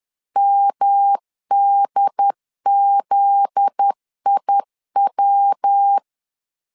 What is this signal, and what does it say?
Morse code, heard as a keyed tone, about 10 wpm, 800 Hz. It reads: MDZIW
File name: LEZIONI / 9 2 10wpm 2736